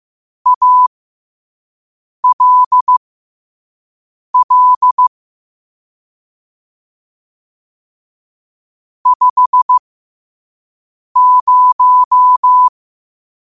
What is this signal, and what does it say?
Morse code, heard as a keyed tone, about 15 wpm, spaced as for 5 wpm, 1000 Hz. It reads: ALL 50
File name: qotdfiles / 5wpm